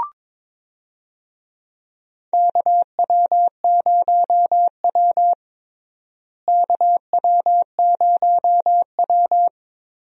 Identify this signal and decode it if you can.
Morse code, heard as a keyed tone, about 22 wpm, 700 Hz. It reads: KW0W KW0W
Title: USA Call Signs - 4 Characters - Repeated 3x - 22wpm